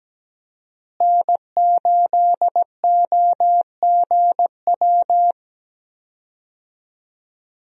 Morse code, heard as a keyed tone, about 17 wpm, 700 Hz. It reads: N8OGW
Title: USA Call Signs - 5 Characters - Repeated 2x - 17wpm